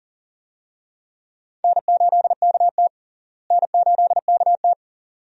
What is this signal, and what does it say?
Morse code, heard as a keyed tone, about 40 wpm, 700 Hz. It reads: N8KT N8KT
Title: USA Call Signs - 4 Characters - Repeated 2x - 40wpm